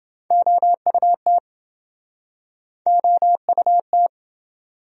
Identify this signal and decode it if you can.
Morse code, heard as a keyed tone, about 28 wpm, 700 Hz. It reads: OUT OUT